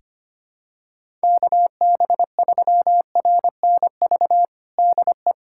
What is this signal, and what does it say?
Morse code, heard as a keyed tone, about 25 wpm, 700 Hz. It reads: KB3RNV DE
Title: USA Callsign DE Callsign 25wpm